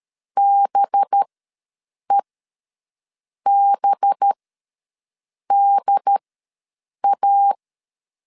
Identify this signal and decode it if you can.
Morse code, heard as a keyed tone, about 13 wpm, spaced as for 6 wpm, 800 Hz. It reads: BEBDA
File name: LEZIONI / 5 LBVC / 5 LB ws 12wpm 3034